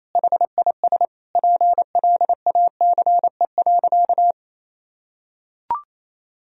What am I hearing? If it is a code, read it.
Morse code, heard as a keyed tone, about 28 wpm, 700 Hz. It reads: HIS PLACE.
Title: Sentences from Top 1000 Words - Review Part 1 - 28wpm